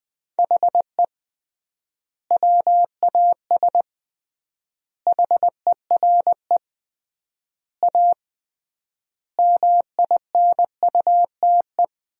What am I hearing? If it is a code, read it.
Morse code, heard as a keyed tone, about 20 wpm, 700 Hz. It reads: HE WAS HERE A MINUTE
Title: 3x Word-Spacing - Sentences from Top 500 Words - 20wpm